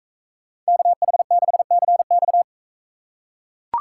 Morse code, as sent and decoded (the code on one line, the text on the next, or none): -.- .... -.... -.-. -..-
KH6CX